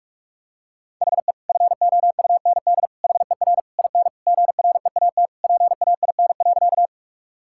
Morse code, heard as a keyed tone, about 41 wpm, 700 Hz. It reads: HE FOUND HER IN GREAT PAIN.
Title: Sentences from Top 1600 Words 45wpm